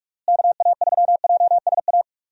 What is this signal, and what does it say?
Morse code, heard as a keyed tone, about 41 wpm, 700 Hz. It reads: KA3JSA
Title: USA Call Signs - 6 Characters - Repeated 3x - 45wpm